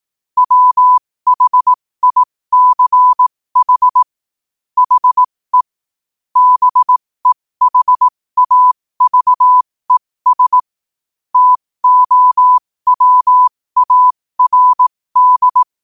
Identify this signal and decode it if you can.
Morse code, heard as a keyed tone, about 18 wpm, spaced as for 15 wpm, 1000 Hz. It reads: WHICH HE BEHAVES TOWARD
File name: qotdfiles / 15wpm